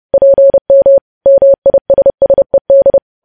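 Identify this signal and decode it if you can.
Morse code, heard as a keyed tone, about 30 wpm, 550 Hz. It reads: PM MISSED
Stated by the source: Morse news headlines sent at 30 wpm, opening with vvv.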